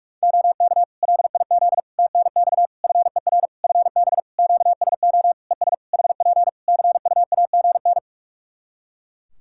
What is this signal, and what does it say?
Morse code, heard as a keyed tone, about 46 wpm, 700 Hz. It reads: OK LIZ TNX FER FB QSO ES HP CUAGN
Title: K1USN SST QSO Practice 45wpm